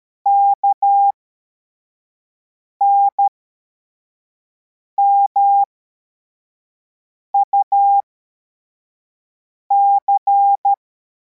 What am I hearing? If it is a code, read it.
Morse code, heard as a keyed tone, about 13 wpm, spaced as for 4 wpm, 800 Hz. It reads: KNMUC
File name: formation / cw / Text44